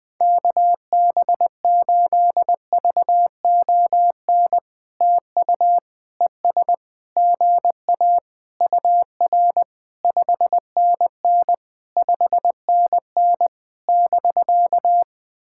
Morse code, heard as a keyed tone, about 21 wpm, 700 Hz. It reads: KB8VON TU ES GA UR 5NN 5NN <BK>
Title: POTA QSO Practice - 20wpm